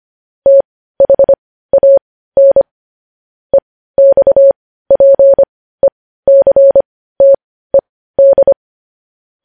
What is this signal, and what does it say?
Morse code, heard as a keyed tone, about 25 wpm, spaced as for 15 wpm, 550 Hz. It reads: THAN EXPECTED
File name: podcast / News headlines-1x-25x15